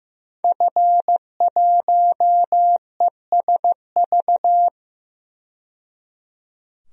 Morse code, heard as a keyed tone, about 15 wpm, 700 Hz. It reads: F1ESV
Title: DXCC-100 Call Signs - 5 Characters - 15wpm